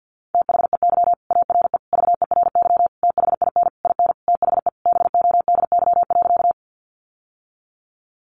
Morse code, heard as a keyed tone, about 48 wpm, 700 Hz. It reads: THEY ARE VERY THIN IN THE BODY.
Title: Sentences from Top 1200 Words 50wpm